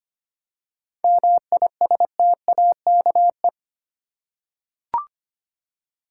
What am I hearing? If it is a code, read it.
Morse code, heard as a keyed tone, about 25 wpm, 700 Hz. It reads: MISTAKE
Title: Seven Letter Words - 25wpm